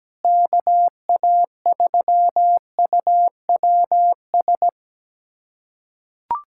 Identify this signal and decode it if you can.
Morse code, heard as a keyed tone, about 17 wpm, 700 Hz. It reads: KA3UWS
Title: USA Callsign DE Callsign 17wpm